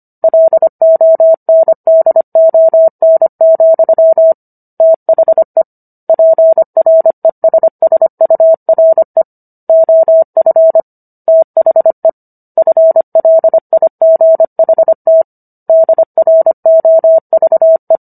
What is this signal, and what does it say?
Morse code, heard as a keyed tone, about 25 wpm, 650 Hz. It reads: LONDON, THE PRESSURE OF THE FLIGHT DROVE